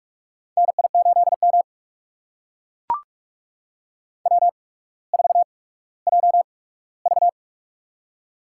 Morse code, heard as a keyed tone, about 44 wpm, spaced as for 19 wpm, 690 Hz. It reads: NI8M W4JV
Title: USA Call Signs - 4 Characters - 8x Character Spacing - 45wpm